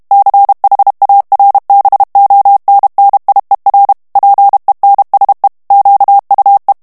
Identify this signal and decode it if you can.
Morse code, heard as a keyed tone, about 32 wpm, 800 Hz. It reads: CHARBONNIER PENSE QUE
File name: morse mp3 / FRANCAIS TEXTE CLAIR 1 VITESSE 30 MOTS MINUTES